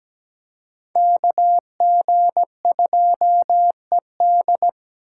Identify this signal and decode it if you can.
Morse code, heard as a keyed tone, about 20 wpm, 700 Hz. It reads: KG2ED
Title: Speed Racing - USA 2x2 Call Signs 17wpm